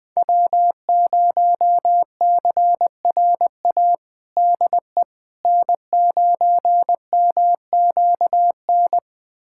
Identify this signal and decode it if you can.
Morse code, heard as a keyed tone, about 20 wpm, 700 Hz. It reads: W0CRA DE N9MQN
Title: USA Callsign DE Callsign - 5 Characters - 20wpm